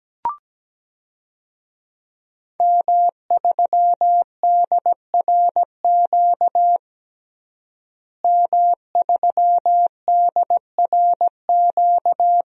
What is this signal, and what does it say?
Morse code, heard as a keyed tone, about 17 wpm, 700 Hz. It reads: M3DRQ M3DRQ
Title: DXCC-100 Call Signs - 5 Characters - Repeated 2x - 17wpm